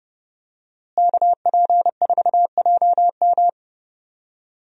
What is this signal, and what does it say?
Morse code, heard as a keyed tone, about 32 wpm, 700 Hz. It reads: KP4JM
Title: DXCC-100 Call Signs - 5 Characters - Repeated 3x - 30wpm